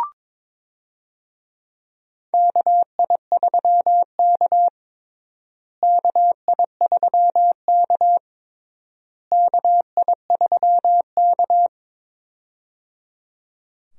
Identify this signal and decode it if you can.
Morse code, heard as a keyed tone, about 22 wpm, 700 Hz. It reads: KI3K KI3K KI3K
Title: USA Call Signs - 4 Characters - Repeated 3x - 22wpm